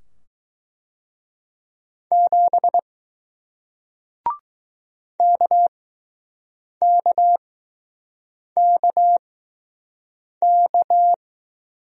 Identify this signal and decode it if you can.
Morse code, heard as a keyed tone, about 20 wpm, spaced as for 6 wpm, 700 Hz. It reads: 7 KKKK